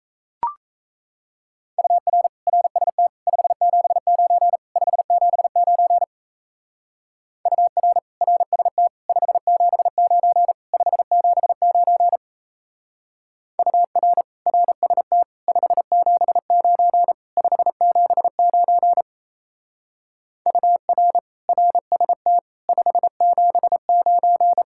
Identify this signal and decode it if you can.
Morse code, heard as a keyed tone, about 35 wpm, 700 Hz. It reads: UR RST 579 579 UR RST 579 579 UR RST 579 579 UR RST 579